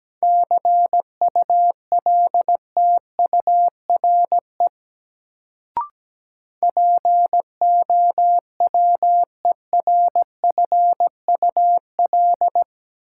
Morse code, heard as a keyed tone, about 17 wpm, 700 Hz. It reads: CULTURE POWERFUL
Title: Sentences from Top 1000 Words - Review Part 6 - New Word Drill - 17wpm